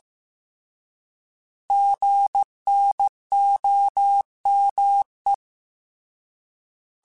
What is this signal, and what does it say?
Morse code, heard as a keyed tone, about 15 wpm, 800 Hz. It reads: GNOME
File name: Alphabetical / Alphabetical mp3s / Gs @13